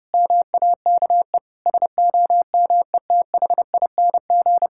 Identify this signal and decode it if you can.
Morse code, heard as a keyed tone, about 30 wpm, 700 Hz. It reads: MAKE SOMETHING
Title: Sentences from Top 1000 Words - Review Part 5 - 30wpm